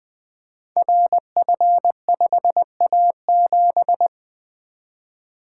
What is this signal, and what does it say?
Morse code, heard as a keyed tone, about 20 wpm, 700 Hz. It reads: RF5A7
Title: Code Groups - 5 Letter:Numbers - 20wpm